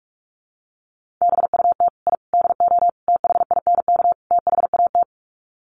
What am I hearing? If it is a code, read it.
Morse code, heard as a keyed tone, about 44 wpm, 690 Hz. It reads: BUT I DO THINK THAT
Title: Sets of 5 Words Encore2 45wpm